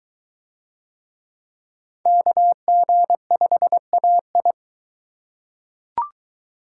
Morse code, heard as a keyed tone, about 20 wpm, 700 Hz. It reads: KG5AI